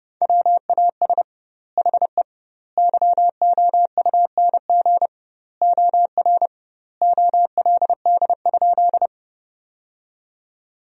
Morse code, heard as a keyed tone, about 30 wpm, 700 Hz. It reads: WAS HE YOUNG OR OLD?
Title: 2x Word-Spacing - Sentences from Top 400 Words - 30wpm